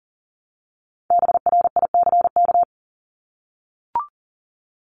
Morse code, heard as a keyed tone, about 40 wpm, 700 Hz. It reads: BRICK
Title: Top 1751-2000 Words 40wpm